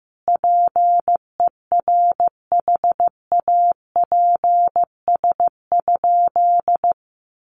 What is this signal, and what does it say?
Morse code, heard as a keyed tone, about 15 wpm, 700 Hz. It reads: PERHAPS?
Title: Sentences from Top 1200 Words 15wpm